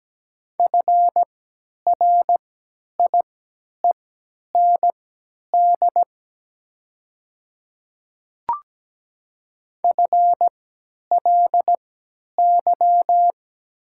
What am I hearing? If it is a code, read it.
Morse code, heard as a keyed tone, about 17 wpm, spaced as for 10 wpm, 700 Hz. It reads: FRIEND FLY
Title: 3x Character Spacing - Top 500 Words - 17wpm